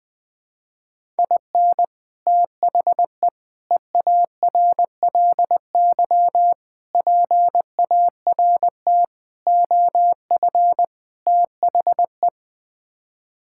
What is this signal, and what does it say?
Morse code, heard as a keyed tone, about 21 wpm, 700 Hz. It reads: IN THE EARLY PART OF THE
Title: Sets of 6 Words - 20wpm